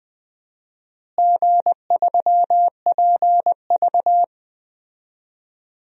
Morse code, heard as a keyed tone, about 21 wpm, 700 Hz. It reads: G3PV
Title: DXCC-100 Call Signs - 4 Characters - Repeated 3x - 20wpm